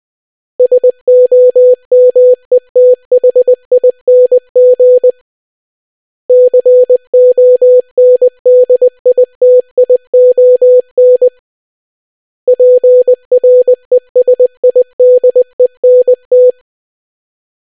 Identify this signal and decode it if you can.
Morse code, heard as a keyed tone, about 20 wpm, 500 Hz. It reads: SOMETHING CONDITION PRESIDENT